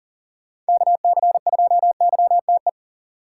Morse code, heard as a keyed tone, about 40 wpm, 700 Hz. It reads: KC2YTE
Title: USA Call Signs - 6 Characters - 2x Character Spacing - 40wpm